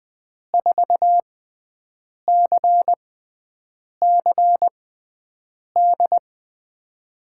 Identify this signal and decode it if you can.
Morse code, heard as a keyed tone, about 20 wpm, spaced as for 7 wpm, 700 Hz. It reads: 4CCD
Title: USA Call Signs - 5 Characters - 6x Character Spacing - 20wpm